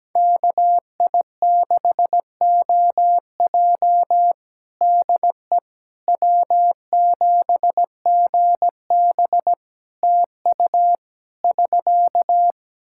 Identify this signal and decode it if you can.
Morse code, heard as a keyed tone, about 17 wpm, 700 Hz. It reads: KI6OJ DE W7GB TU <SK>
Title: Practice QSO - Exchange 4 - Ending - 17wpm